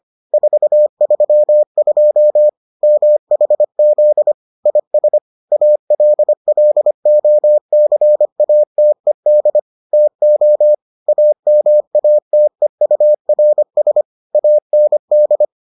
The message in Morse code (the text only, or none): 432 MHZ IS ALLOCATED TO AMATEURS AND